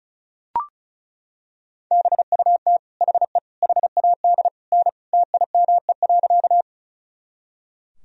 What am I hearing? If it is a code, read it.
Morse code, heard as a keyed tone, about 34 wpm, 700 Hz. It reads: BUT HE HAD N TIME.
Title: Sentences from Top 100 Words and Abbreviations 35wpm